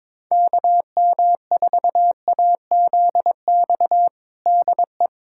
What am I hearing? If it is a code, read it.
Morse code, heard as a keyed tone, about 22 wpm, 700 Hz. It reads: KM4AZX DE
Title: USA Callsign DE Callsign - 6 Characters - 22wpm